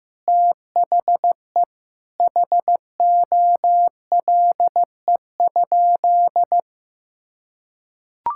Morse code, heard as a keyed tone, about 16 wpm, 700 Hz. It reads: THE HOLE?
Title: Sentences from Top 1600 Words 15wpm